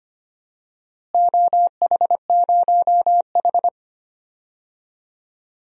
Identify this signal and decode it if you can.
Morse code, heard as a keyed tone, about 25 wpm, 700 Hz. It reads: OH0H
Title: DXCC-100 Call Signs - 4 Characters - Repeated 2x - 25wpm